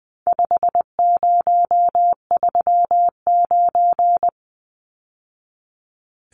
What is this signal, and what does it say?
Morse code, heard as a keyed tone, about 20 wpm, 700 Hz. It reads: 5039
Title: Numbers - 4-Digit 20wpm